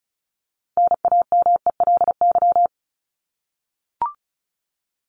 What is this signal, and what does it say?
Morse code, heard as a keyed tone, about 34 wpm, 700 Hz. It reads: NAMELY
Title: LY Suffix - Top 100 Words with Base Words 35wpm